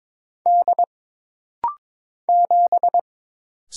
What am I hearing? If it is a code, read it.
Morse code, heard as a keyed tone, about 22 wpm, 700 Hz. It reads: D 7
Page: This rapid-fire timing is geared to push your ICR (Instant Character Recognition) skills.